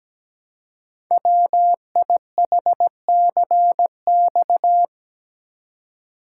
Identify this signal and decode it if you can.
Morse code, heard as a keyed tone, about 17 wpm, 700 Hz. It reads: WIHCX
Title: Code Groups - 5 Letters - 17wpm